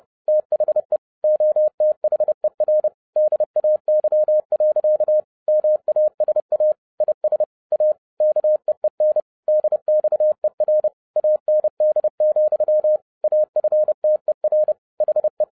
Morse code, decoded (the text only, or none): THE OTHER DAY. MASA IS A KEEN DXER AND, AFTER HE